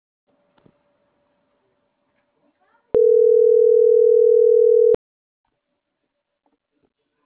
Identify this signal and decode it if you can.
Ringback tone (North America), 440 + 480 Hz.